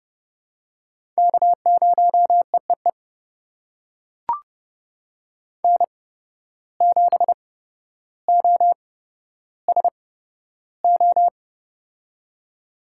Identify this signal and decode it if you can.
Morse code, heard as a keyed tone, about 29 wpm, spaced as for 13 wpm, 700 Hz. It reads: K0EEE N7OSO
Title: USA Call Signs - 5 Characters - 8x Character Spacing - 30wpm